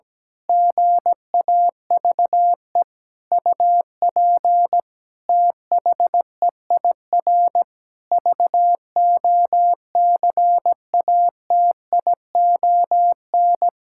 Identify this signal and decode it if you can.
Morse code, heard as a keyed tone, about 17 wpm, 700 Hz. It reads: GAVE UP THEIR VOCATION
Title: Pangram Challenge v2 - 17wpm